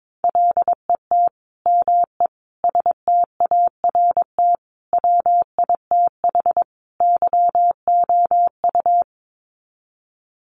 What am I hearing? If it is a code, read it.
Morse code, heard as a keyed tone, about 22 wpm, 700 Hz. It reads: LET ME START WITH YOU
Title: Sets of 5 Words Encore 22wpm